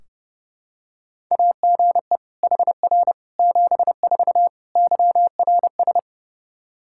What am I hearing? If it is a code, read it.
Morse code, heard as a keyed tone, about 30 wpm, 700 Hz. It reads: AGE HR 74 YRS
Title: Practice QSO - Exchange 3 - More Info - 30wpm